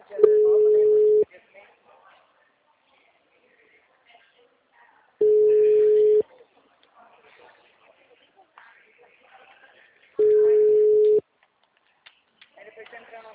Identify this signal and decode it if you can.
Ringback tone (Europe, 425 Hz), with speech.